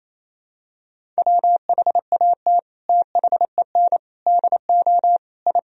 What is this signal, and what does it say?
Morse code, heard as a keyed tone, about 28 wpm, 700 Hz. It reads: WHAT THEN DO I
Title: Sentences from Top 1000 Words - Review Part 2 - 28wpm